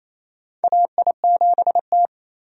Morse code, heard as a keyed tone, about 28 wpm, 700 Hz. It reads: AI7T
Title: USA Call Signs - 4 Characters - 28wpm